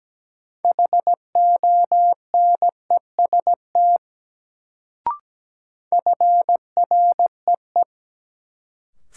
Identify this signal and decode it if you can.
Morse code, heard as a keyed tone, about 17 wpm, 700 Hz. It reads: HONEST FREE
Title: Sets of 5 Words Encore - New Word Drill - 17wpm